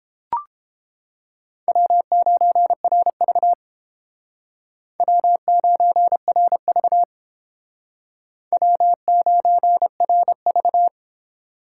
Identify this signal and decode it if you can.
Morse code, heard as a keyed tone, about 30 wpm, 700 Hz. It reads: W9RV W9RV W9RV